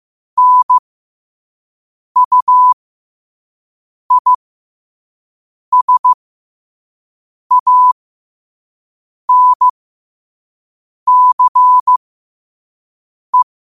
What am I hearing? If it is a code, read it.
Morse code, heard as a keyed tone, about 15 wpm, spaced as for 5 wpm, 1000 Hz. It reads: NUISANCE
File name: qotdfiles / 5wpm